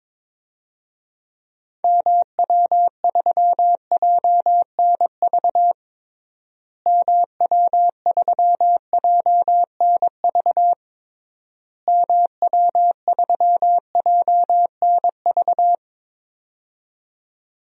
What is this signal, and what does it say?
Morse code, heard as a keyed tone, about 22 wpm, 700 Hz. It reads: MW3JNV MW3JNV MW3JNV
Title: DXCC-100 Call Signs - 6 Characters - Repeated 3x - 22wpm